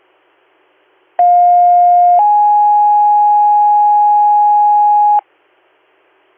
Two-tone page, heard as a one-second tone, then a three-second tone, about 710 then 830 Hz. It is heard from 1.2 to 5.2 s.